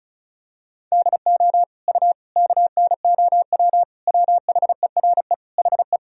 Morse code, heard as a keyed tone, about 36 wpm, 700 Hz. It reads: DO U KNOW WHERE HE
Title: Sentences from Top 300 Words and Abbreviations 35wpm